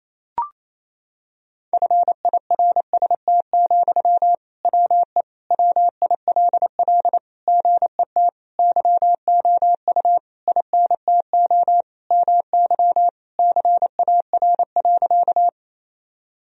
Morse code, heard as a keyed tone, about 28 wpm, 700 Hz. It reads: FIRST, WE WILL GET YOU INTO MY CAR.
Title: Sentences from Top 1000 Words - Review Part 1 - 28wpm